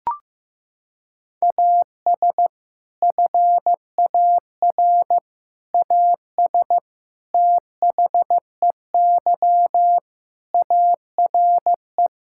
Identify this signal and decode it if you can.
Morse code, heard as a keyed tone, about 15 wpm, 700 Hz. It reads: AS FAR AS THEY ARE
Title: Sentences from Top 1000 Words - Review Part 6 - 15wpm